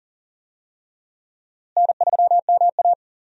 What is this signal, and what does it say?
Morse code, heard as a keyed tone, about 40 wpm, 700 Hz. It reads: N3MA
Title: USA Call Signs - 4 Characters - Repeated 3x - 40wpm